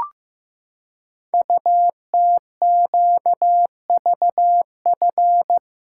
Morse code, heard as a keyed tone, about 15 wpm, 700 Hz. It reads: UTQVF
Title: Code Groups - 5 Letters - 15wpm